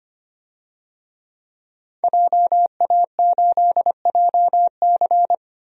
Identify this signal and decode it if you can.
Morse code, heard as a keyed tone, about 25 wpm, 700 Hz. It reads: JA8JC
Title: DXCC-100 Call Signs - 5 Characters - Repeated 2x - 25wpm